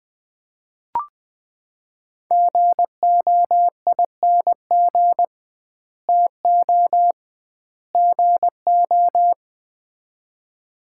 Morse code, heard as a keyed tone, about 21 wpm, 700 Hz. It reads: GOING TO GO
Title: 2x Word-Spacing - Sets of 3 Words - 20wpm